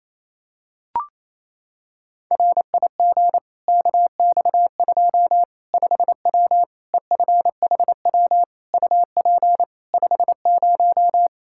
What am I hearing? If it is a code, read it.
Morse code, heard as a keyed tone, about 27 wpm, 700 Hz. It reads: RIG KX2 5W EFHW UP 50
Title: Practice QSO - Exchange 2 - Info - 28wpm